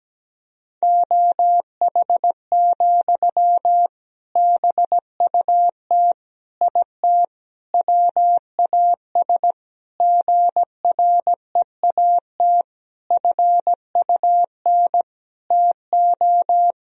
Morse code, heard as a keyed tone, about 17 wpm, 700 Hz. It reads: OH, BUT IT WAS GREAT FUN TO
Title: Sentences from Top 1000 Words - Review Part 4 - 17wpm